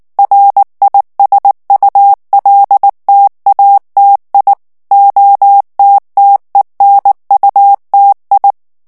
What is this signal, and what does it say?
Morse code, heard as a keyed tone, about 19 wpm, 800 Hz. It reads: RISULTATI OTTENUTI
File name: morse mp3 / ITALIEN texte clair 2 vitesse 18 mots minute